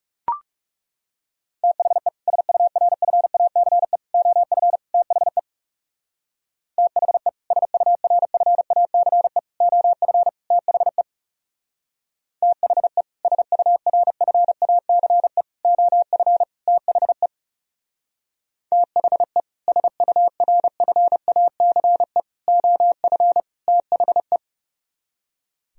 Morse code, heard as a keyed tone, about 36 wpm, 700 Hz. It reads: THE SURFACE OF THE THE SURFACE OF THE THE SURFACE OF THE THE SURFACE OF THE
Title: Speed-Racing - Sets of 4 Words 30wpm